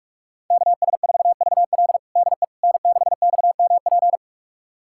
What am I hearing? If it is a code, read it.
Morse code, heard as a keyed tone, about 45 wpm, 700 Hz. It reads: KS4VL DE N6XMP
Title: USA Callsign DE Callsign - 5 Characters - 45wpm